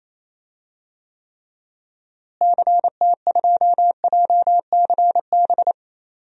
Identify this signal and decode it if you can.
Morse code, heard as a keyed tone, about 27 wpm, 700 Hz. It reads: CT2JCB
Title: DXCC-100 Call Signs - 6 Characters - Repeated 2x - 28wpm